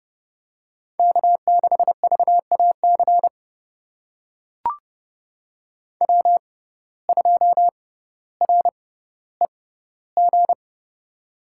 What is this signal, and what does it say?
Morse code, heard as a keyed tone, about 30 wpm, spaced as for 15 wpm, 700 Hz. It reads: K6VAC W2REG
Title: USA Call Signs - 5 Characters - 6x Character Spacing - 30wpm